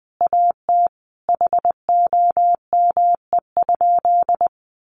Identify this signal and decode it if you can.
Morse code, heard as a keyed tone, about 20 wpm, 700 Hz. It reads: AT HOME?
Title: Sentences from Top 1100 Words 20wpm